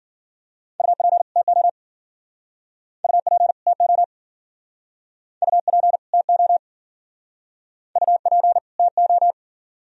Morse code, heard as a keyed tone, about 47 wpm, 700 Hz. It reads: UP TO UP TO UP TO UP TO
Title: Speed-Racing - Sets of 2 Words 40wpm